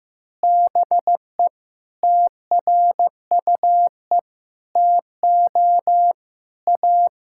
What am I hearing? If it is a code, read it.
Morse code, heard as a keyed tone, about 16 wpm, 700 Hz. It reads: BE TRUE TO A
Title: Sentences from Top 1600 Words 15wpm